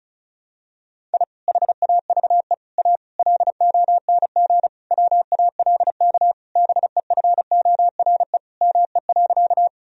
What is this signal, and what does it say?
Morse code, heard as a keyed tone, about 36 wpm, 700 Hz. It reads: I HAVE A LONG WALK BEFORE ME.
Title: Sentences from Top 1000 Words - Review Part 1 - 35wpm